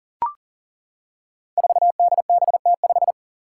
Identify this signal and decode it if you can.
Morse code, heard as a keyed tone, about 39 wpm, 700 Hz. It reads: 4DBT5
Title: Five Challenging Letter-Number Combinations 40wpm